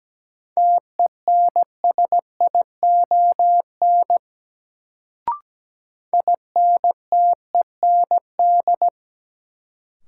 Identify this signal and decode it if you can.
Morse code, heard as a keyed tone, about 17 wpm, 700 Hz. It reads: TENSION INTEND
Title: Sentences from Top 1600 Words - New Word Drill Alt - 17wpm